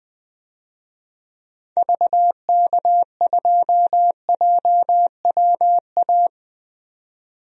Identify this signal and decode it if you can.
Morse code, heard as a keyed tone, about 20 wpm, 700 Hz. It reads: VK2JWA
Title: DXCC-100 Call Signs - 6 Characters - 20wpm